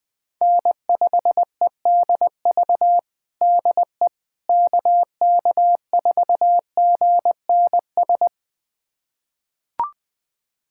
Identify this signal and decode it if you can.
Morse code, heard as a keyed tone, about 20 wpm, 700 Hz. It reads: N5EDV DE KK4GNS
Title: USA Callsign DE Callsign 20wpm